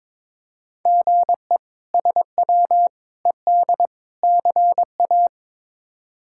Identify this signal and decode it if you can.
Morse code, heard as a keyed tone, about 22 wpm, 700 Hz. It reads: GE SW ED CA
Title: K1USN SST QSO Practice - 22wpm